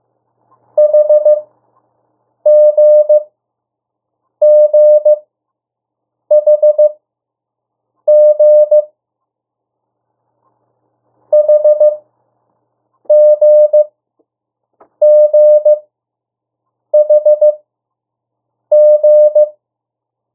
Morse code, heard as a keyed tone, about 15 wpm, spaced as for 6 wpm, 600 Hz. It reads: HGGHG HGGHG
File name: morse mp3 / 02 urok 1